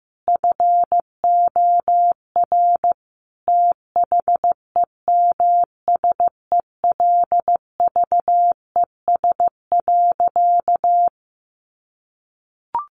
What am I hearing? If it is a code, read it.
Morse code, heard as a keyed tone, about 15 wpm, 700 Hz. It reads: FOR THEMSELVES.
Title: Sentences from Top 1400 Words 15wpm